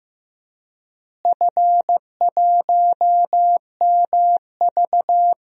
Morse code, heard as a keyed tone, about 15 wpm, 700 Hz. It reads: F1MV
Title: DXCC-100 Call Signs - 4 Characters - Repeated 2x - 15wpm